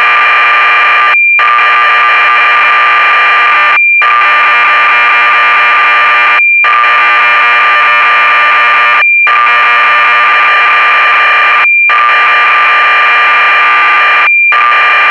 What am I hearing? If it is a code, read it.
Cassette data (Kansas City type).